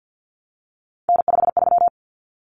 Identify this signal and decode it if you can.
Morse code, heard as a keyed tone, about 50 wpm, 700 Hz. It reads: N53
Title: Common Cut Numbers - 3-Digit 50wpm